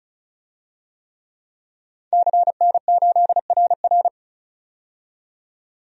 Morse code, heard as a keyed tone, about 36 wpm, 700 Hz. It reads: CN8RR
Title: DXCC-100 Call Signs - 5 Characters - 35wpm